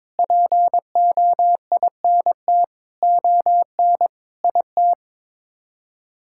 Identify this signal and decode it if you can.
Morse code, heard as a keyed tone, about 21 wpm, 700 Hz. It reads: POINT ON IT
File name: Sets of 9 Words - 22wpm